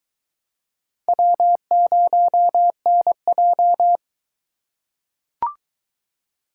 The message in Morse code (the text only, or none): W0NJ